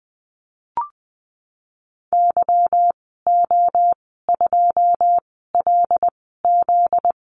Morse code, heard as a keyed tone, about 20 wpm, spaced as for 14 wpm, 700 Hz. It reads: YO2LZ
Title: 245 Lesson 39g - Letter z - Call Signs 20wpm